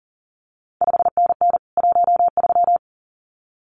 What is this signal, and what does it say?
Morse code, heard as a keyed tone, about 39 wpm, 700 Hz. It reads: 5NN 13
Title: CQ WW DX Contest Exchange Encore - 40wpm